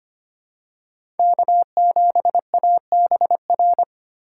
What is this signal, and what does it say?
Morse code, heard as a keyed tone, about 25 wpm, 700 Hz. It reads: K7ABR
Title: USA Call Signs - 5 Characters - 2x Character Spacing - 25wpm